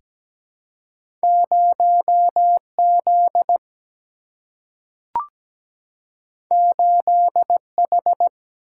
Morse code, heard as a keyed tone, about 17 wpm, 700 Hz. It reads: 0Z 8H
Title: Single-Letter Single-Number 17wpm